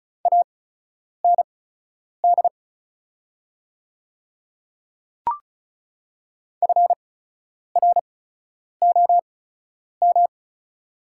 Morse code, heard as a keyed tone, about 36 wpm, spaced as for 10 wpm, 690 Hz. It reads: AND FROM
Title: 8x Character Spacing - Top 100 Words - 35wpm